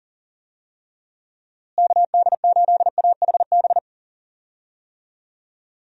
Morse code, heard as a keyed tone, about 40 wpm, 700 Hz. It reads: KD8AHB
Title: USA Call Signs - 6 Characters - Repeated 2x - 40wpm